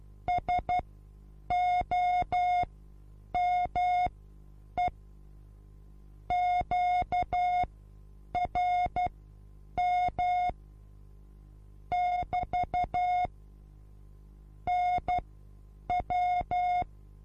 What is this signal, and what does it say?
Morse code, heard as a keyed tone, about 12 wpm, spaced as for 8 wpm, 720 Hz. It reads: SOME QRM <BT> NW
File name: Onlinelehrgang / morsen / DJ4UF Morsen23